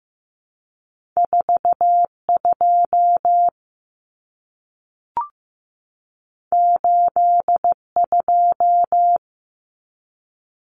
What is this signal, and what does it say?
Morse code, heard as a keyed tone, about 15 wpm, 700 Hz. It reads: 4 2 8 2
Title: Common Cut Numbers - 2-Digit 15wpm